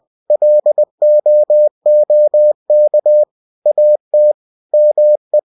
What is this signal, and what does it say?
Morse code, heard as a keyed tone, about 20 wpm, 600 Hz. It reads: LOOK AT ME
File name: cw-books / winnie-the-pooh / winnie-20wpm-page-0066